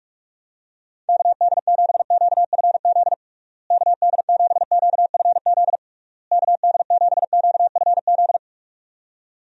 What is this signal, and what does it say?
Morse code, heard as a keyed tone, about 41 wpm, 700 Hz. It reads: KD7QFZ KD7QFZ KD7QFZ
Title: USA Call Signs - 6 Characters - Repeated 3x - 45wpm